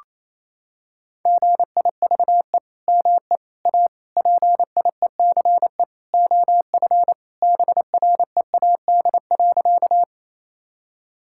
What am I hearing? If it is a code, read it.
Morse code, heard as a keyed tone, about 28 wpm, 700 Hz. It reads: GIVE ME A PIECE OF BREAD.
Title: Sentences from Top 1600 Words 28wpm